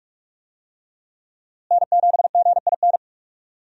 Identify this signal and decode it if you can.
Morse code, heard as a keyed tone, about 45 wpm, 700 Hz. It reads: N7GIN
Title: USA Call Signs - 5 Characters - Repeated 3x - 45wpm